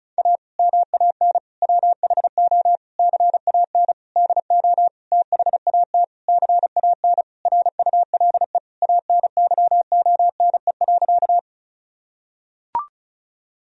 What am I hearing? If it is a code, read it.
Morse code, heard as a keyed tone, about 34 wpm, 700 Hz. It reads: A MAN WHO CAN DO THAT CAN RULE ANYONE.
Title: Sentences from Top 1000 Words - Review Part 2 - 35wpm